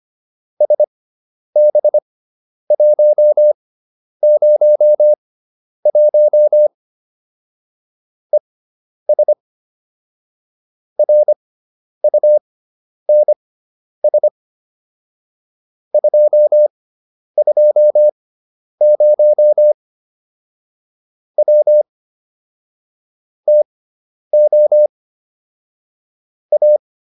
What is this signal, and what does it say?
Morse code, heard as a keyed tone, about 25 wpm, spaced as for 10 wpm, 600 Hz. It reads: SB101 ES RUNS 220 W TO A
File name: QSO310 10